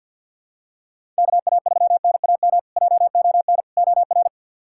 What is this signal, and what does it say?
Morse code, heard as a keyed tone, about 47 wpm, 700 Hz. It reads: KA3NAM JON OR
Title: K1USN SST QSO Practice - 50wpm